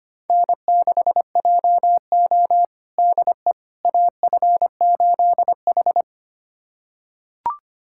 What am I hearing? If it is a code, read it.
Morse code, heard as a keyed tone, about 25 wpm, 700 Hz. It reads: N6JO DE AF8H
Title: USA Callsign DE Callsign - 4 Characters - 25wpm